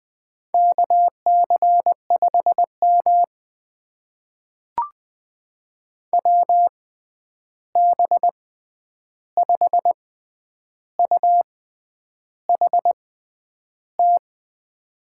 Morse code, heard as a keyed tone, about 20 wpm, spaced as for 9 wpm, 700 Hz. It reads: KC5M WB5UHT
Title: USA Call Signs - 6x Character Spacing - 20wpm